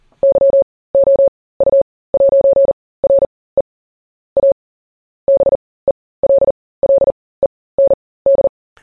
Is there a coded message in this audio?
Yes: Morse code from 0.2 to 8.5 s, heard as a keyed tone, about 40 wpm, spaced as for 20 wpm, 550 Hz. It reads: YOU'RE A BELLEND